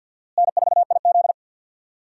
Morse code, heard as a keyed tone, about 46 wpm, 700 Hz. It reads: N4IZ
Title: USA Call Signs - 4 Characters - Repeated 2x - 50wpm